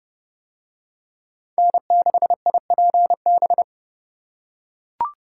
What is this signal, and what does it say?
Morse code, heard as a keyed tone, about 29 wpm, 700 Hz. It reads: N6IPB